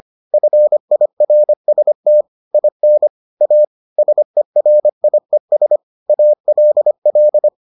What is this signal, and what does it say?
Morse code, heard as a keyed tone, about 25 wpm, 600 Hz. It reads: FIRST IN A SERIES ALL